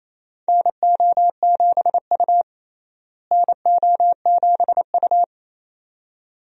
Morse code, heard as a keyed tone, about 28 wpm, 700 Hz. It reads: NO7U NO7U
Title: USA Call Signs - 4 Characters - Repeated 2x - 28wpm